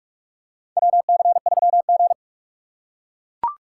Morse code, heard as a keyed tone, about 41 wpm, 700 Hz. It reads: WC3G
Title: USA Call Signs - 4 Characters - 2x Character Spacing - 45wpm